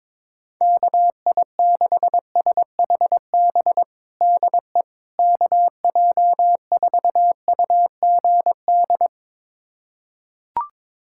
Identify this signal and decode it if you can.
Morse code, heard as a keyed tone, about 22 wpm, 700 Hz. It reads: KI6SHB DE KJ4UGD
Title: USA Callsign DE Callsign 22wpm